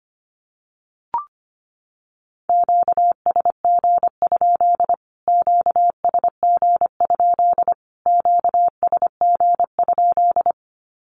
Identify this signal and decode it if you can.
Morse code, heard as a keyed tone, about 25 wpm, 700 Hz. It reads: QSG? QSG? QSG?
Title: Q-Codes Repeated 3x 25wpm